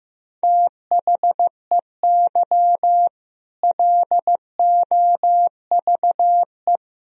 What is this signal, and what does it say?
Morse code, heard as a keyed tone, about 15 wpm, 700 Hz. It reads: THEY LOVE
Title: Sentences from Top 1000 Words - Review Part 3 - 15wpm